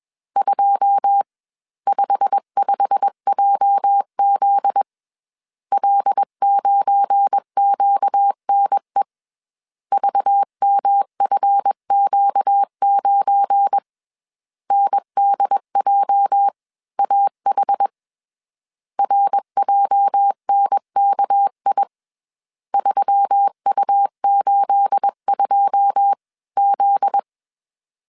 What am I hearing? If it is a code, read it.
Morse code, heard as a keyed tone, about 21 wpm, spaced as for 19 wpm, 800 Hz. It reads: 255JZ L9QNE VMFQ9 NDJAH RJNKI 3U82Z
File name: CW / DET / 1 9 1500C S C 20wpm 2310